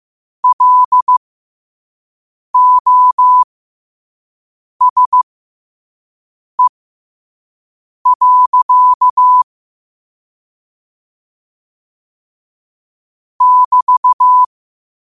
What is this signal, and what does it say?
Morse code, heard as a keyed tone, about 15 wpm, spaced as for 5 wpm, 1000 Hz. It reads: LOSE. <BT>
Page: New quotes every day in morse code at 5 Words per minute.